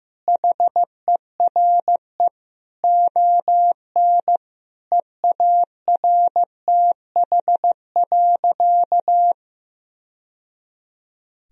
Morse code, heard as a keyed tone, about 16 wpm, 700 Hz. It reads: HERE ON EARTH.
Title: Sentences from Top 1000 Words - Review Part 2 - 15wpm